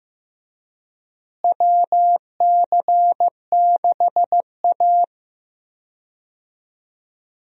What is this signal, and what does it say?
Morse code, heard as a keyed tone, about 15 wpm, 700 Hz. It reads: WC6A
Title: USA Call Signs - 4 Characters - Repeated 3x - 15wpm